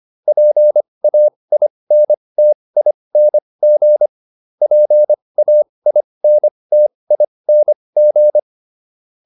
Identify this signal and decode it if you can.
Morse code, heard as a keyed tone, about 25 wpm, spaced as for 20 wpm, 600 Hz. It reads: PAINTING PAINTING